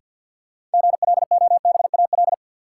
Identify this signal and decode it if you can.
Morse code, heard as a keyed tone, about 51 wpm, 700 Hz. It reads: GLOBAL
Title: Sentences from Top 1200 Words - New Word Drill Alt - 50wpm